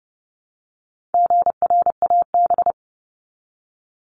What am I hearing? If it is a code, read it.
Morse code, heard as a keyed tone, about 30 wpm, 700 Hz. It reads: GRAB
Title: Top 1751-2000 Words 30wpm